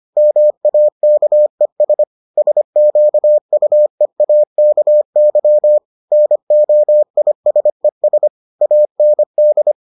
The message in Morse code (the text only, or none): MAKES SQUEAKY NOISES AND